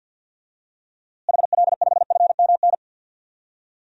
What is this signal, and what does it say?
Morse code, heard as a keyed tone, about 52 wpm, 700 Hz. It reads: HL5FMN
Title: DXCC-100 Call Signs - 6 Characters - Repeated 2x - 50wpm